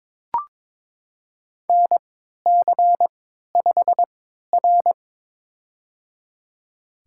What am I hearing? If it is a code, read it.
Morse code, heard as a keyed tone, about 22 wpm, spaced as for 12 wpm, 700 Hz. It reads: NC5R
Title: USA Call Signs - 4 Characters - 3x Character Spacing - 22wpm